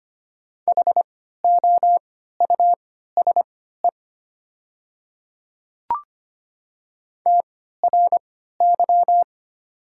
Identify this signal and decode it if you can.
Morse code, heard as a keyed tone, about 25 wpm, spaced as for 14 wpm, 700 Hz. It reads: HOUSE TRY
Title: 3x Character Spacing - Top 200 Words - 25wpm